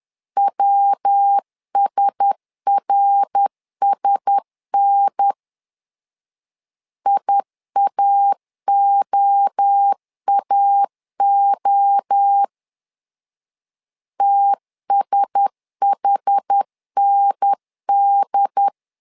Morse code, heard as a keyed tone, about 10 wpm, 800 Hz. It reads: WSRSN IAOAO TSHND
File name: LEZIONI / 4 Q1S 10wpm 2133